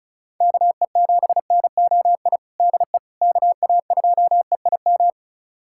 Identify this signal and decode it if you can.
Morse code, heard as a keyed tone, about 34 wpm, 700 Hz. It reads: KE7NOI DE KA2EIM
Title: USA Callsign DE Callsign - 6 Characters - 35wpm